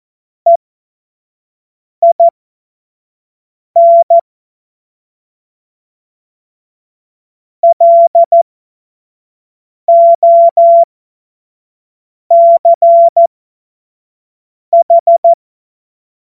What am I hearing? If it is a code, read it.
Morse code, heard as a keyed tone, about 14 wpm, spaced as for 5 wpm, 680 Hz. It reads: EIN LOCH
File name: Morsekurs / Witze / Witz 02. November 2021, 25 F70